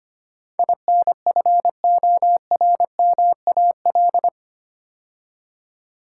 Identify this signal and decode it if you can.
Morse code, heard as a keyed tone, about 25 wpm, 690 Hz. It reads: INFORMAL
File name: 8 Letter Words - First 3 Letters Given - 25wpm